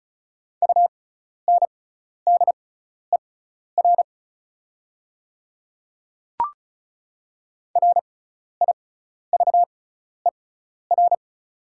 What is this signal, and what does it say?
Morse code, heard as a keyed tone, about 36 wpm, spaced as for 12 wpm, 700 Hz. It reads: UNDER RIVER
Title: 6x Character Spacing - Top 500 Words - 35wpm